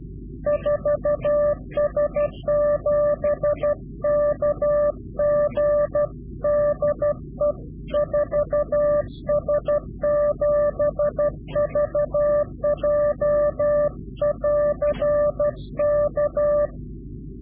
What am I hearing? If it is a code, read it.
Morse code, heard as a keyed tone, about 12 wpm, 590 Hz. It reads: 4S7KG DE 4S7VJ <AR> K
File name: CW-QSO / Directional Call / Directional QSO